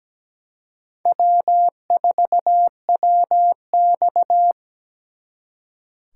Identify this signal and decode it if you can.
Morse code, heard as a keyed tone, about 20 wpm, 700 Hz. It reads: W4WX
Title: Speed Racing - USA 1x2 Call Signs 17wpm